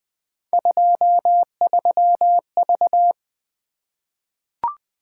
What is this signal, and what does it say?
Morse code, heard as a keyed tone, about 20 wpm, 700 Hz. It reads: 23V
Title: Three Letter-Number Combinations - 20wpm